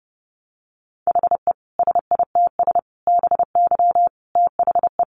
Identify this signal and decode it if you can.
Morse code, heard as a keyed tone, about 30 wpm, 700 Hz. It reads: HE SITS BY THE
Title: Easy Phrases 30wpm